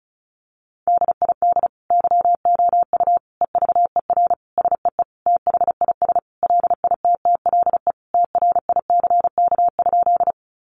Morse code, heard as a keyed tone, about 34 wpm, 700 Hz. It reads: DID YOU EVER SEE THIS LITTLE TRICK?
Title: Sentences from Top 1500 Words 35wpm